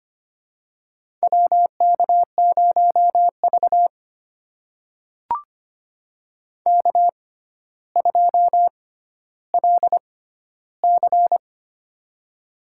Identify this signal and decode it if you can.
Morse code, heard as a keyed tone, about 25 wpm, spaced as for 13 wpm, 700 Hz. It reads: WK0V K2LC
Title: USA Call Signs - 4 Characters - 6x Character Spacing - 25wpm